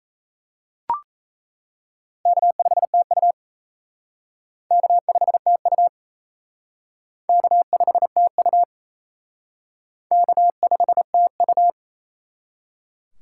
Morse code, heard as a keyed tone, about 34 wpm, 700 Hz. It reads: K5TU K5TU K5TU K5TU
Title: Speed Racing - USA 1x2 Call Signs 28wpm